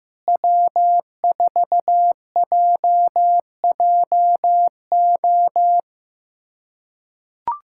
Morse code, heard as a keyed tone, about 15 wpm, 700 Hz. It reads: W4JJO
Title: USA Callsign DE Callsign - 5 Characters - 15wpm